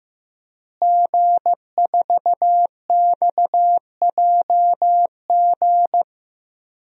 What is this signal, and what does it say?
Morse code, heard as a keyed tone, about 15 wpm, 700 Hz. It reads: G4XJG
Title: DXCC-100 Call Signs - 5 Characters - Repeated 3x - 15wpm